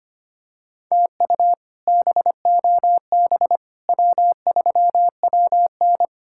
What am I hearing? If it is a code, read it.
Morse code, heard as a keyed tone, about 25 wpm, 700 Hz. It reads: TU BOB W3WN
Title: K1USN SST QSO Practice - 25wpm